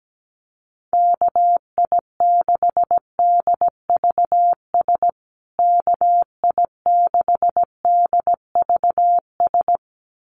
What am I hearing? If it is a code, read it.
Morse code, heard as a keyed tone, about 17 wpm, 700 Hz. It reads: KI6DVS KI6DVS
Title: USA Call Signs Repeated 2x 17wpm